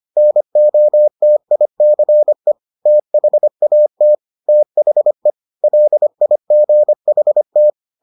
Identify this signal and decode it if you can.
Morse code, heard as a keyed tone, about 25 wpm, 600 Hz. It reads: NOTICE THAT THE LIGHT